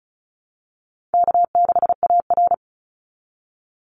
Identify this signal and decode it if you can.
Morse code, heard as a keyed tone, about 34 wpm, 700 Hz. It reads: K6AR
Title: Call Signs Repeated 2x - 35wpm